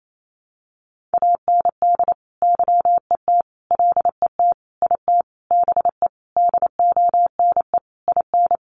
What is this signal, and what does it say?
Morse code, heard as a keyed tone, about 28 wpm, 700 Hz. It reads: AND YET LET IT BE DONE IN
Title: Sentences from Top 1200 Words 28wpm